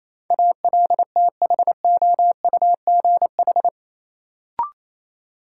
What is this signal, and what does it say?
Morse code, heard as a keyed tone, about 28 wpm, 700 Hz. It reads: ALTHOUGH
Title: Sentences from Top 600 Words - New Word Drill - 28wpm